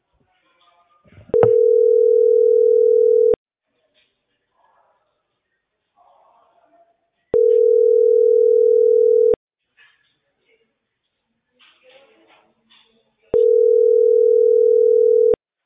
Ringback tone (North America), 440 + 480 Hz.